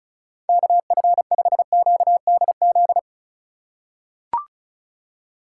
Morse code, heard as a keyed tone, about 36 wpm, 700 Hz. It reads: KF5QDZ
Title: USA Call Signs - 6 Characters - 3x Character Spacing - 35wpm